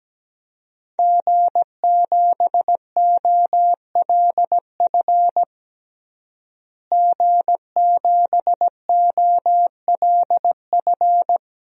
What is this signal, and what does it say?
Morse code, heard as a keyed tone, about 17 wpm, 700 Hz. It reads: G7OLF G7OLF
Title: DXCC-100 Call Signs - 5 Characters - Repeated 2x - 17wpm